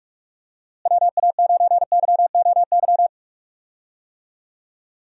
Morse code, heard as a keyed tone, about 44 wpm, 700 Hz. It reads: WA9YOY
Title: USA Call Signs - 6 Characters - Repeated 2x - 45wpm